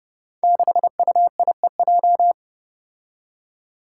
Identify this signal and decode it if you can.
Morse code, heard as a keyed tone, about 30 wpm, 700 Hz. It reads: 6UIEJ
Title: Five Challenging Letter-Number Combinations 30wpm